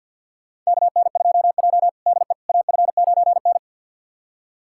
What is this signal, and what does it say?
Morse code, heard as a keyed tone, about 52 wpm, 700 Hz. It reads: KN2J DE AF9N
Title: USA Callsign DE Callsign - 4 Characters - 50wpm